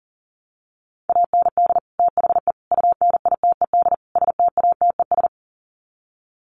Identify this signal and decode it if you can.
Morse code, heard as a keyed tone, about 39 wpm, 700 Hz. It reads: AND THE UNITED STATES
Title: Sets of 4 Words 40wpm